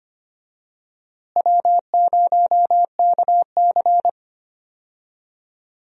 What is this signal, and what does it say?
Morse code, heard as a keyed tone, about 25 wpm, 700 Hz. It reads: W0KC
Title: USA Call Signs - 4 Characters - Repeated 3x - 25wpm